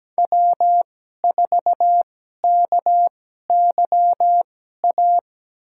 Morse code, heard as a keyed tone, about 17 wpm, spaced as for 12 wpm, 700 Hz. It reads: W4KYA
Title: USA Call Signs - 2x Character Spacing - 17wpm